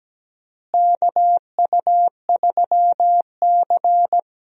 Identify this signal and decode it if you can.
Morse code, heard as a keyed tone, about 17 wpm, 700 Hz. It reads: KU3C
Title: USA Call Signs - 4 Characters - Repeated 3x - 17wpm